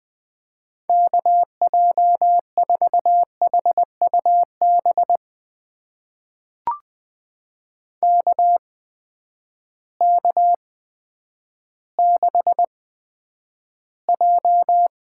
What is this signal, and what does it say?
Morse code, heard as a keyed tone, about 20 wpm, spaced as for 10 wpm, 700 Hz. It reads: KJ4HUB KK6J
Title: USA Call Signs - 6 Characters - 8x Character Spacing - 20wpm